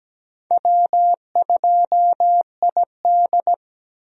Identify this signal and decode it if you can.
Morse code, heard as a keyed tone, about 17 wpm, 700 Hz. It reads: W2ID
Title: USA Call Signs - 4 Characters - Repeated 3x - 17wpm